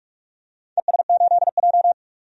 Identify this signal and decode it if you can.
Morse code, heard as a keyed tone, about 44 wpm, 700 Hz. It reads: ES8J
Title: DXCC-100 Call Signs - 4 Characters - 45wpm